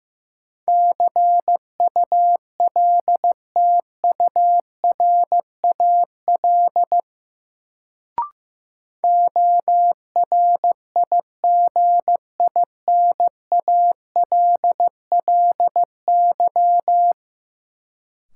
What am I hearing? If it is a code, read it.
Morse code, heard as a keyed tone, about 15 wpm, 700 Hz. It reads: CULTURAL ORIGINALLY
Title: Sentences from Top 1500 Words - New Word Drill - 15wpm